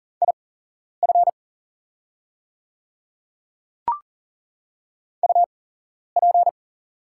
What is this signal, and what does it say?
Morse code, heard as a keyed tone, about 40 wpm, spaced as for 11 wpm, 700 Hz. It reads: IF UP
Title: 8x Character Spacing - Top 100 Words - 40wpm